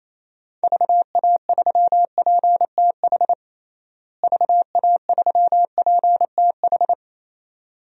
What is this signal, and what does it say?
Morse code, heard as a keyed tone, about 28 wpm, 700 Hz. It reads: VA3PTH VA3PTH
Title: DXCC-100 Call Signs - 6 Characters - Repeated 2x - 28wpm